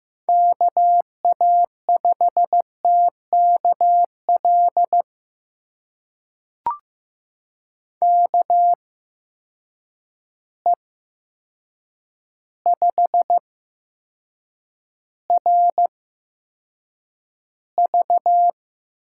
Morse code, heard as a keyed tone, about 15 wpm, spaced as for 7 wpm, 700 Hz. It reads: KA5TKL KE5RV
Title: USA Call Signs - 6 Characters - 8x Character Spacing - 15wpm